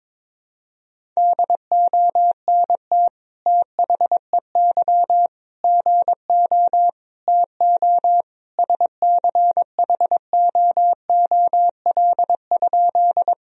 Morse code, heard as a keyed tone, about 22 wpm, 700 Hz. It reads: DONT THEY GO TO SCHOOL?
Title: Sentences from Top 1000 Words - Review Part 1 - 22wpm